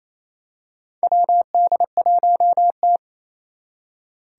Morse code, heard as a keyed tone, about 27 wpm, 700 Hz. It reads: WD1T
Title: Four Challenging Letter-Number Combinations 28wpm